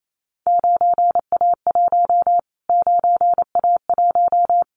Morse code, heard as a keyed tone, about 28 wpm, 700 Hz. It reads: 9A1 9A1
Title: Callsign Prefixes for Top 25 DXCC - Repeated 3x - 28wpm